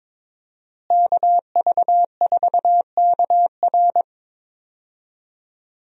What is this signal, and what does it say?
Morse code, heard as a keyed tone, about 21 wpm, 700 Hz. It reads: KV4KR
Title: USA Call Signs - 5 Characters - Repeated 3x - 22wpm